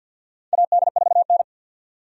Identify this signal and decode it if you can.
Morse code, heard as a keyed tone, about 47 wpm, 700 Hz. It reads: AD4N
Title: USA Call Signs - 4 Characters - 50wpm